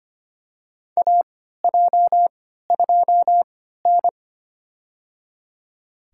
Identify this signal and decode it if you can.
Morse code, heard as a keyed tone, about 25 wpm, spaced as for 14 wpm, 700 Hz. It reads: AJ2N
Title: USA Call Signs - 4 Characters - 3x Character Spacing - 25wpm